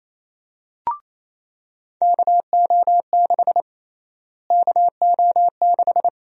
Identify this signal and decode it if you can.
Morse code, heard as a keyed tone, about 27 wpm, 700 Hz. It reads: KO6 KO6
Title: USA Call Sign Prefixes - Repeated 2x - 28wpm